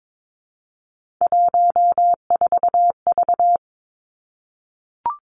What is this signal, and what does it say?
Morse code, heard as a keyed tone, about 22 wpm, 700 Hz. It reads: 14V
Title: Three Challenging Letter-Number Combinations 22wpm